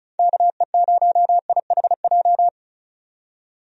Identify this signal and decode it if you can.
Morse code, heard as a keyed tone, about 36 wpm, 700 Hz. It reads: KE0IHJ
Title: USA Call Signs - 6 Characters - 35wpm